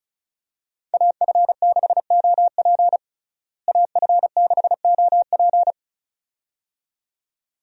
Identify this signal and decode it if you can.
Morse code, heard as a keyed tone, about 36 wpm, 700 Hz. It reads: AF6OP AF6OP
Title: USA Call Signs - 5 Characters - Repeated 2x - 35wpm